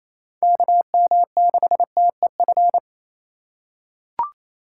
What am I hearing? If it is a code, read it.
Morse code, heard as a keyed tone, about 28 wpm, 700 Hz. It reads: KM6TEF
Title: USA Call Signs - 6 Characters - 4x Character Spacing - 28wpm